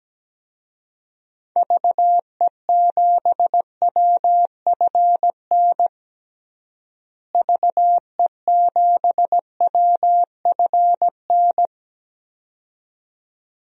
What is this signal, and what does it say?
Morse code, heard as a keyed tone, about 17 wpm, 700 Hz. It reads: VE7WFN VE7WFN
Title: DXCC-100 Call Signs - 6 Characters - Repeated 2x - 17wpm